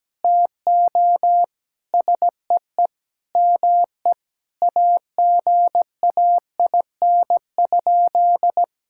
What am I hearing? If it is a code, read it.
Morse code, heard as a keyed tone, about 17 wpm, 700 Hz. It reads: TO SEE ME AGAIN?
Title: Sentences from Top 1600 Words 17wpm